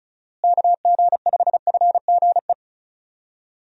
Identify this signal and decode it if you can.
Morse code, heard as a keyed tone, about 34 wpm, 700 Hz. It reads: KG5FGE
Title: USA Call Signs - 6 Characters - Repeated 2x - 35wpm